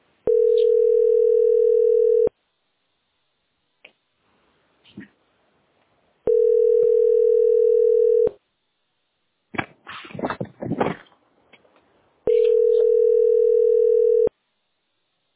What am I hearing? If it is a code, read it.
Ringback tone (North America), 440 + 480 Hz.